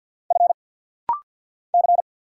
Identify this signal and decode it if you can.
Morse code, heard as a keyed tone, about 50 wpm, 700 Hz. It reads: F C
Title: Single Letters - Rapid-Fire 50wpm